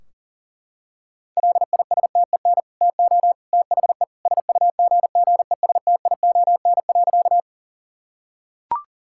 Morse code, heard as a keyed tone, about 37 wpm, 700 Hz. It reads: LISTEN TO THE SUGGESTION.
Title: Sentences from Top 1600 Words 40wpm